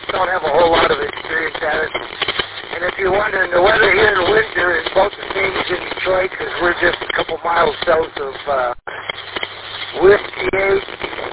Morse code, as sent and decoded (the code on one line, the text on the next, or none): none